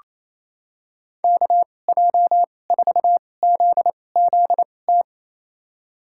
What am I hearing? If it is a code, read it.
Morse code, heard as a keyed tone, about 28 wpm, spaced as for 20 wpm, 700 Hz. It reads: KJ4ZZT
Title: USA Call Signs - 2x Character Spacing - 28wpm